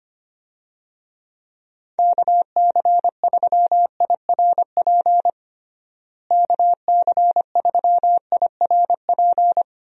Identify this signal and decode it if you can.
Morse code, heard as a keyed tone, about 25 wpm, 700 Hz. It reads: KC3IRP KC3IRP
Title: USA Call Signs - 6 Characters - Repeated 3x - 25wpm